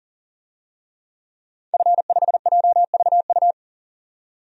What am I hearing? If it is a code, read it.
Morse code, heard as a keyed tone, about 40 wpm, 700 Hz. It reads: F5JVU
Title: DXCC-100 Call Signs - 5 Characters - 40wpm